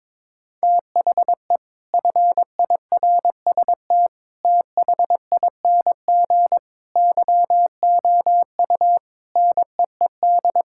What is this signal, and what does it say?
Morse code, heard as a keyed tone, about 22 wpm, 700 Hz. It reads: THE FIRST THING YOU NEED
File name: practice / 22wpm / Sentences with 8-Word Phrases - 22wpm